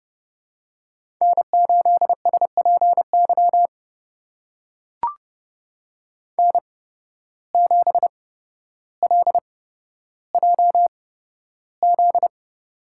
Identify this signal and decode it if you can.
Morse code, heard as a keyed tone, about 30 wpm, spaced as for 13 wpm, 700 Hz. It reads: N8SPY N7LJZ
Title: USA Call Signs - 5 Characters - 8x Character Spacing - 30wpm